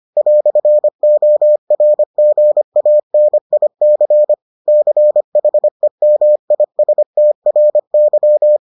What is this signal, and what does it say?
Morse code, heard as a keyed tone, about 25 wpm, 600 Hz. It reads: "ORGANIC CHEMISTRY